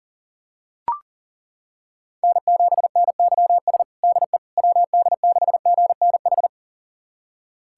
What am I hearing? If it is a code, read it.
Morse code, heard as a keyed tone, about 40 wpm, 700 Hz. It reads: N7NYS DE WD6GNH
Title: USA Callsign DE Callsign 40wpm